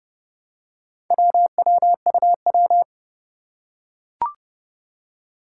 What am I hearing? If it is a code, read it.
Morse code, heard as a keyed tone, about 30 wpm, 700 Hz. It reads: WWUW
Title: Four Challenging Letter-Number Combinations 30wpm